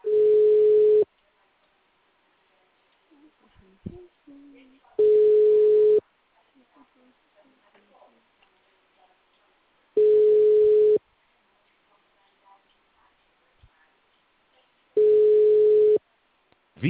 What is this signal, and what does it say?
Ringback tone (Europe, 425 Hz), with speech.